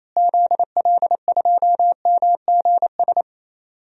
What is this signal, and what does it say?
Morse code, heard as a keyed tone, about 28 wpm, 700 Hz. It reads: ZL2MGS
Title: DXCC-100 Call Signs - 6 Characters - Repeated 2x - 28wpm